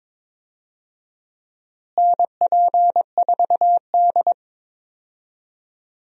Morse code, heard as a keyed tone, about 21 wpm, 700 Hz. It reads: NP4D
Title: USA Call Signs - 4 Characters - Repeated 3x - 22wpm